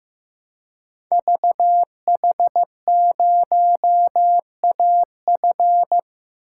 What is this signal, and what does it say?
Morse code, heard as a keyed tone, about 15 wpm, 700 Hz. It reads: VH0AF
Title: Code Groups - 5 Letter:Numbers - 15wpm